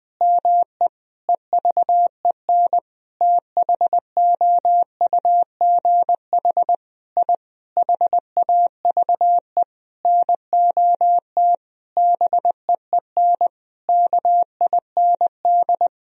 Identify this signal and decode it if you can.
Morse code, heard as a keyed tone, about 21 wpm, 700 Hz. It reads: ME EVEN THOUGH I HAVE NOT BEEN KIND
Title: Sentences from Top 1000 Words - Review Part 2 - 20wpm